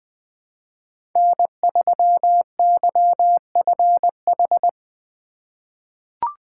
Morse code, heard as a keyed tone, about 20 wpm, 700 Hz. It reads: N3YFH
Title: USA Call Signs - 5 Characters - 4x Character Spacing - 20wpm